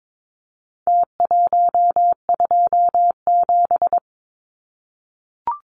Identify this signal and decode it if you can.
Morse code, heard as a keyed tone, about 22 wpm, 700 Hz. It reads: T127
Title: Numbers - Contest Style - 4-Digit 22wpm